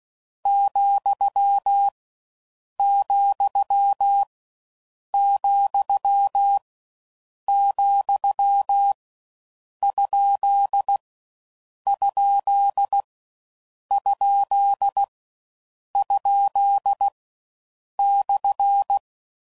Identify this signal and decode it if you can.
Morse code, heard as a keyed tone, about 16 wpm, 800 Hz. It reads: , , , , ? ? ? ? /